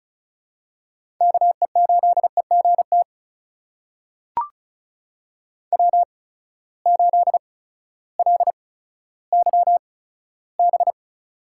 Morse code, heard as a keyed tone, about 36 wpm, spaced as for 16 wpm, 700 Hz. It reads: KE8EGT W8LYB
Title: USA Call Signs - 8x Character Spacing - 35wpm